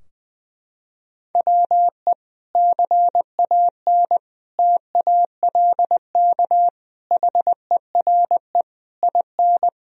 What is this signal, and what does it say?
Morse code, heard as a keyed tone, about 20 wpm, 700 Hz. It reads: WE CAN TALK HERE IN
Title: Sentences from Top 1000 Words - Review Part 4 - 20wpm